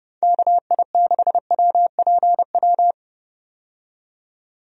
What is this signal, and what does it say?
Morse code, heard as a keyed tone, about 30 wpm, 700 Hz. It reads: KI6WPW
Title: USA Call Signs - 6 Characters - 30wpm